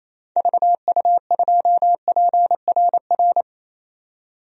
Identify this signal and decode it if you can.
Morse code, heard as a keyed tone, about 27 wpm, 700 Hz. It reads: VU2PRR
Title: DXCC-100 Call Signs - 6 Characters - 28wpm